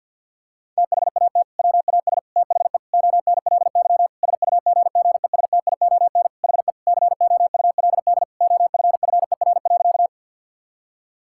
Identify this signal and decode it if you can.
Morse code, heard as a keyed tone, about 50 wpm, 700 Hz. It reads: THAT WAS THE ONLY SUGGESTION HE COULD OFFER.
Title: Sentences from Top 1600 Words 50wpm